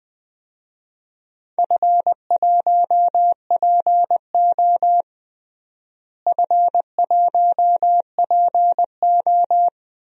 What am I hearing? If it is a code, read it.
Morse code, heard as a keyed tone, about 20 wpm, 700 Hz. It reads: F1PO F1PO
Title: DXCC-100 Call Signs - 4 Characters - Repeated 2x - 20wpm